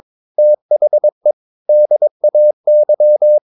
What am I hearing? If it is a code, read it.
Morse code, heard as a keyed tone, about 22 wpm, 600 Hz. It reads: THE DAY
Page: Romanen Never af Ken Follet på engelsk.